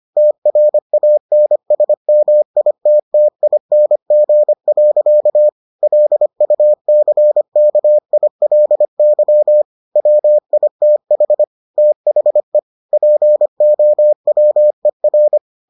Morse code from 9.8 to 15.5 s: WITH THE POWER

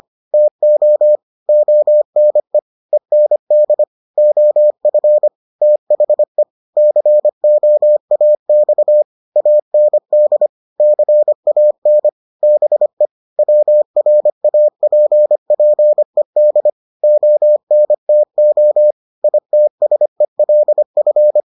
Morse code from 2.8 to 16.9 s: END OF THE COAX AND CAN BE WRAPPED

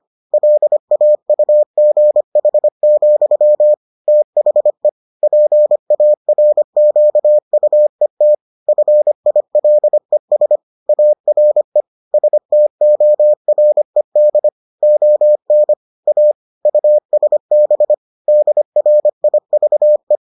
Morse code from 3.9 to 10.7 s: THE PARQUET FILES

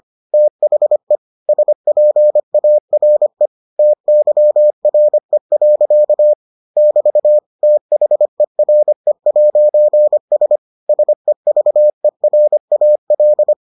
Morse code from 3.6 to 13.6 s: TYRE. <BT> THERE'S SEVERAL

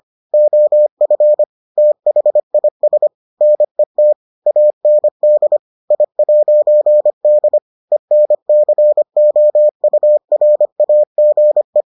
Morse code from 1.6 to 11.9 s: THIS NET AND I'D ENCOURAGE

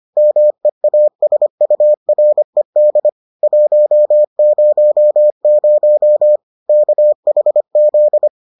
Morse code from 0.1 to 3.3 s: MEASURED

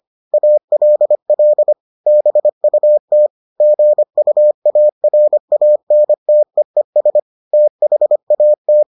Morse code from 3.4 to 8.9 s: GUARANTEES THAT

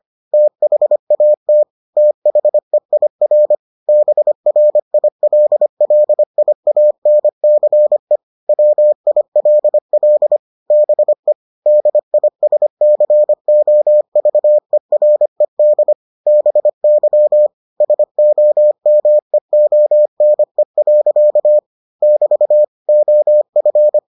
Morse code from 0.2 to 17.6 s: THAT THEIR BRILLIANCE WILL BE DISCOVERED BY

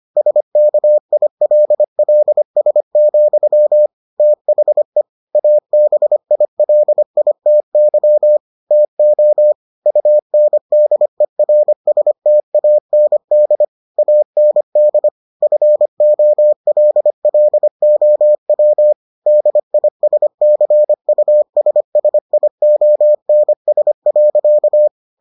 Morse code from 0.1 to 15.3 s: SKILLS, THE ABILITY TO UNDERSTAND AND

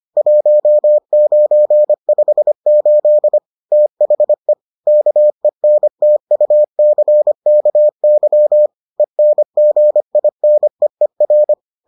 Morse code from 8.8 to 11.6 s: ENGINEER

